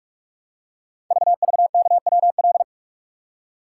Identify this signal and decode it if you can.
Morse code, heard as a keyed tone, about 44 wpm, 700 Hz. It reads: VVKWL
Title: Code Groups - 5 Letters - 45wpm